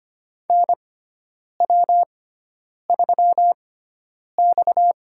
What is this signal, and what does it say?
Morse code, heard as a keyed tone, about 25 wpm, spaced as for 9 wpm, 700 Hz. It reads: NW3X
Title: USA Call Signs - 4 Characters - 6x Character Spacing - 25wpm